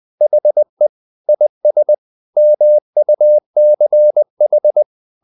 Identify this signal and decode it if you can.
Morse code, heard as a keyed tone, about 20 wpm, 600 Hz. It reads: HE IS MUCH
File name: the old man and the sea part 2 20wpm